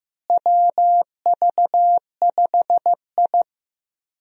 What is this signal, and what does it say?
Morse code, heard as a keyed tone, about 15 wpm, 700 Hz. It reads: WV5I
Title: USA Call Signs - 4 Characters - Repeated 2x - 15wpm